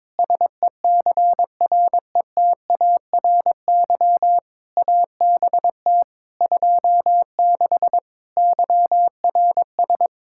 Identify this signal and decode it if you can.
Morse code, heard as a keyed tone, about 22 wpm, 700 Hz. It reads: SECRETARY ABT 26 YRS
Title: K1USN SST QSO Practice 22wpm